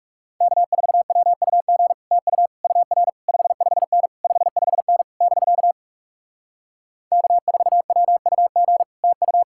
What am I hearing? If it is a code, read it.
Morse code, heard as a keyed tone, about 40 wpm, 700 Hz. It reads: K4WUG TU UR 55N 55N <BK> K4WUG TU